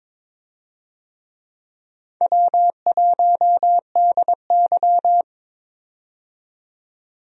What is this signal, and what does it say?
Morse code, heard as a keyed tone, about 22 wpm, 700 Hz. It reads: W1DY
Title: USA Call Signs - 4 Characters - 22wpm